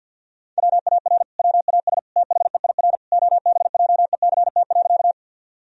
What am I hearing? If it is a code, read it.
Morse code, heard as a keyed tone, about 50 wpm, 700 Hz. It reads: WAR WAS THEIR OBJECT.
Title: Sentences from Top 1000 Words - Review Part 3 - 50wpm